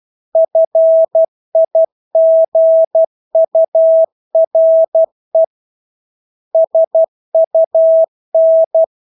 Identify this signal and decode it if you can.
Morse code, heard as a keyed tone, about 12 wpm, 650 Hz. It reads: FIGURE SUN
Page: Engelske ord 12 wpm